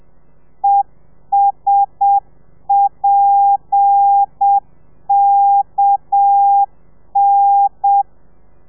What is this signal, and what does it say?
Morse code, heard as a keyed tone, about 7 wpm, 800 Hz. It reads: ESPKN
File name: telegrafia / i2tfj / Lez7 30